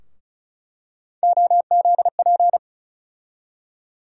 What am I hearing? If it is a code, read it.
Morse code, heard as a keyed tone, about 34 wpm, 700 Hz. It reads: OZP
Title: USA Call Sign Postfixes - 3 Letters 35wpm